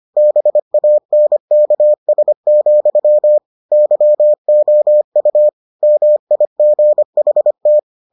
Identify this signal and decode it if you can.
Morse code, heard as a keyed tone, about 25 wpm, 600 Hz. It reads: BANKS, YOU MIGHT